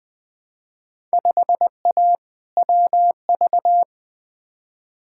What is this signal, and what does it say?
Morse code, heard as a keyed tone, about 20 wpm, 700 Hz. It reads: 5A WV
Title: Field Day Exchanges - 20wpm